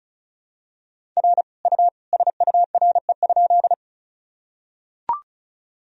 Morse code, heard as a keyed tone, about 34 wpm, 700 Hz. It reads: R U SURE?
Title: Sentences from Top 300 Words and Abbreviations 35wpm